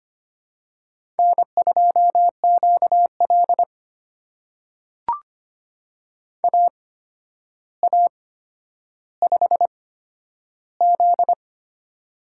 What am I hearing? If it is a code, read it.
Morse code, heard as a keyed tone, about 25 wpm, spaced as for 11 wpm, 700 Hz. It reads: N2QL AA5Z
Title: USA Call Signs - 4 Characters - 8x Character Spacing - 25wpm